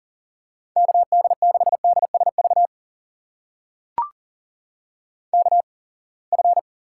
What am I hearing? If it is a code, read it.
Morse code, heard as a keyed tone, about 40 wpm, spaced as for 28 wpm, 700 Hz. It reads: KD6DSV KF
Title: USA Call Signs - 6 Characters - 8x Character Spacing - 40wpm